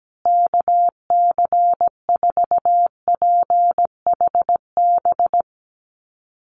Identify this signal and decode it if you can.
Morse code, heard as a keyed tone, about 17 wpm, 700 Hz. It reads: KC4PHB
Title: USA Call Signs Repeated 2x 17wpm